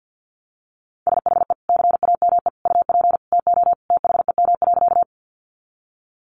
Morse code, heard as a keyed tone, about 48 wpm, 700 Hz. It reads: SHE CAME UP TO THEM.
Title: Sentences from Top 200 Words 50wpm